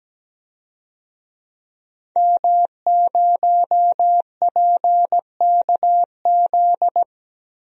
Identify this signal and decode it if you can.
Morse code, heard as a keyed tone, about 17 wpm, 700 Hz. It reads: M0PKZ
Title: DXCC-100 Call Signs - 5 Characters - 17wpm